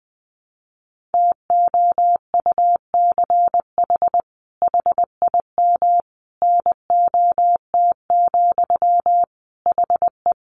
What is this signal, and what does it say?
Morse code, heard as a keyed tone, about 19 wpm, 700 Hz. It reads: TOUCH HIM NOT, HE
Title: Sentences from Top 1200 Words 20wpm